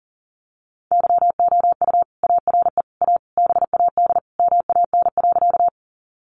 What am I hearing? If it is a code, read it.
Morse code, heard as a keyed tone, about 40 wpm, 700 Hz. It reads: YOU ARE A BAD MAN.
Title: Sentences from Top 700 Words 40wpm